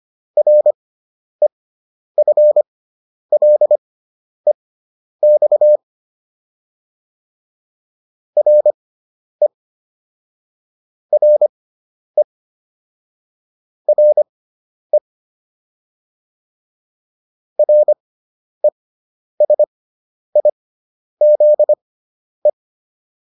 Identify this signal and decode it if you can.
Morse code, heard as a keyed tone, about 25 wpm, spaced as for 10 wpm, 600 Hz. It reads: REFLEX RE RE RE RESIZE
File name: RE6 10